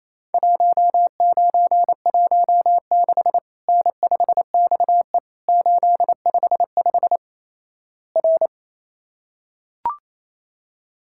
Morse code, heard as a keyed tone, about 28 wpm, 700 Hz. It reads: 1916 N5XE 855 R
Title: Worked All Europe DX Contest - QTC Exchange - 28wpm